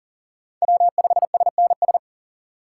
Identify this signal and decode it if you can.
Morse code, heard as a keyed tone, about 40 wpm, 700 Hz. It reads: W5SNS
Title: USA Call Signs - 5 Characters - 4x Character Spacing - 40wpm